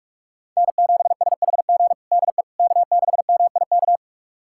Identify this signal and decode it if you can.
Morse code, heard as a keyed tone, about 41 wpm, 700 Hz. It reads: N7SHG DE K6MIK
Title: USA Callsign DE Callsign - 5 Characters - 45wpm